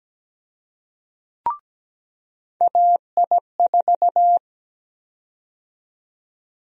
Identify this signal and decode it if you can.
Morse code, heard as a keyed tone, about 17 wpm, 700 Hz. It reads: AI4
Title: USA Call Sign Prefixes - 17wpm